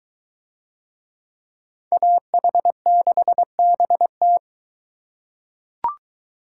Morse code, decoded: AH6BT

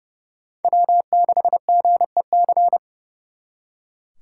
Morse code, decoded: W6GEC

